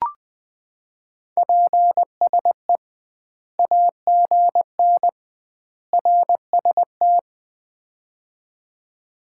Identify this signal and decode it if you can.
Morse code, heard as a keyed tone, about 20 wpm, 700 Hz. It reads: PSE AGN RST
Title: Common QSO Phrases - 2x Word-Spacing - 20wpm